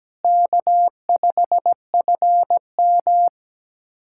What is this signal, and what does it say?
Morse code, heard as a keyed tone, about 17 wpm, 700 Hz. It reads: K5FM
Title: USA Callsign DE Callsign - 4 Characters - 17wpm